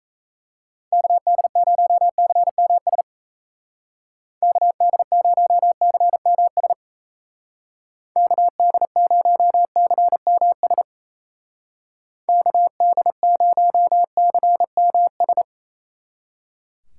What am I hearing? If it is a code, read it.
Morse code, heard as a keyed tone, about 35 wpm, 700 Hz. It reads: KD0CMS KD0CMS KD0CMS KD0CMS
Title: Speed Racing - USA Call Signs 28wpm